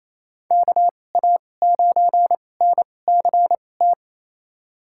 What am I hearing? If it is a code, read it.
Morse code, heard as a keyed tone, about 27 wpm, spaced as for 20 wpm, 700 Hz. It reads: KA9NCT
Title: USA Call Signs - 2x Character Spacing - 28wpm